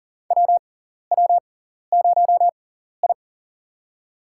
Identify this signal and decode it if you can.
Morse code, heard as a keyed tone, about 37 wpm, spaced as for 14 wpm, 700 Hz. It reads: WW0I
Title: USA Call Signs - 6x Character Spacing - 40wpm